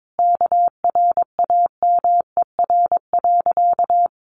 Morse code, heard as a keyed tone, about 22 wpm, 700 Hz. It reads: KRAMER.
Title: A Sound Of Thunder 22wpm